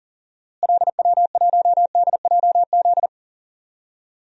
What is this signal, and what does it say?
Morse code, heard as a keyed tone, about 40 wpm, 700 Hz. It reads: LW1DJZ
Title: DXCC-100 Call Signs - 6 Characters - Repeated 2x - 40wpm